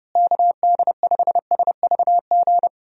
Morse code, heard as a keyed tone, about 30 wpm, 700 Hz. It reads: KD5SVG
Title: USA Call Signs - 6 Characters - Repeated 2x - 30wpm